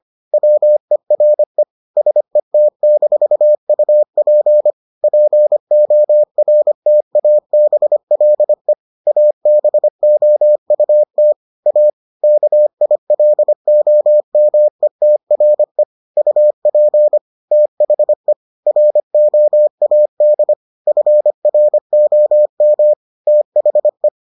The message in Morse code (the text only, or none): WERE SET-UP PORTABLE ABOUT A KILOMETRE UP THE ROAD FROM THE